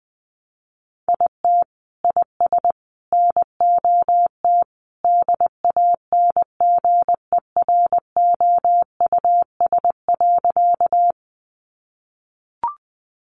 Morse code, heard as a keyed tone, about 20 wpm, 700 Hz. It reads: IT IS NOT DANGEROUS.
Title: Sentences from Top 1200 Words 20wpm